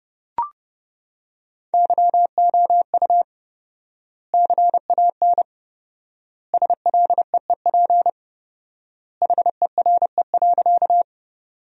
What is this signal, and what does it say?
Morse code, heard as a keyed tone, about 30 wpm, 700 Hz. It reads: YOU CAN SLEEP HERE.
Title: 4x Word-Spacing - Sentences from Top 500 Words - 30wpm